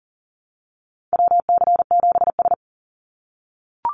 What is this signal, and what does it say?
Morse code, heard as a keyed tone, about 37 wpm, 700 Hz. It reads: WC7S
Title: Call Signs - 40wpm